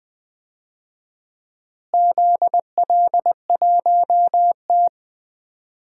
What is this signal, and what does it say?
Morse code, heard as a keyed tone, about 20 wpm, 700 Hz. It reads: ZL1T
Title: DXCC-100 Call Signs - 4 Characters - Repeated 2x - 20wpm